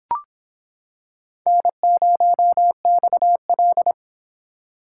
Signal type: Morse code

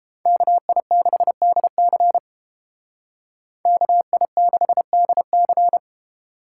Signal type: Morse code

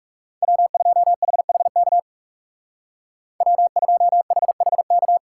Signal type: Morse code

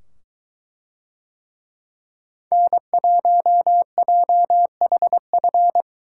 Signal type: Morse code

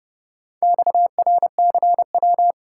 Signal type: Morse code

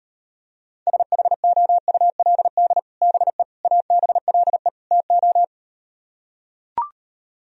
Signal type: Morse code

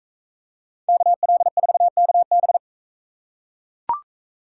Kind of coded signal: Morse code

